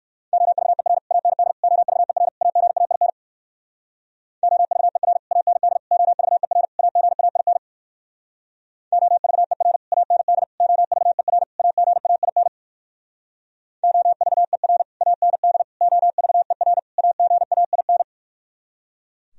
Morse code, heard as a keyed tone, about 65 wpm, 700 Hz. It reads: OVER AND OVER AGAIN OVER AND OVER AGAIN OVER AND OVER AGAIN OVER AND OVER AGAIN